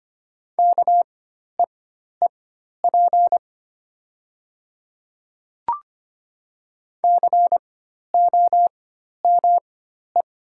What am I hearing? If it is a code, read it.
Morse code, heard as a keyed tone, about 25 wpm, spaced as for 12 wpm, 700 Hz. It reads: KEEP COME
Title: 4x Character Spacing - Top 300 Words - 25wpm